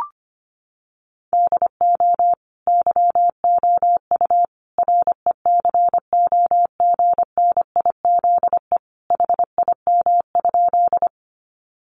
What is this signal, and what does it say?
Morse code, heard as a keyed tone, about 25 wpm, 700 Hz. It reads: DO YOU RECOGNIZE HIM?
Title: Sentences from Top 1400 Words 25wpm